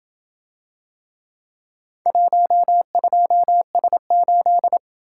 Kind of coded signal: Morse code

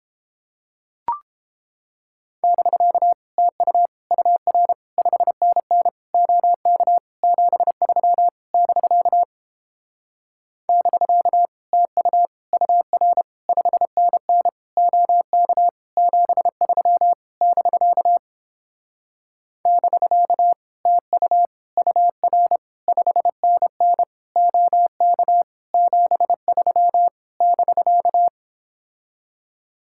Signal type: Morse code

